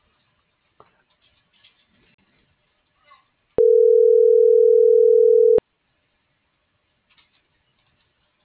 Ringback tone (North America), 440 + 480 Hz.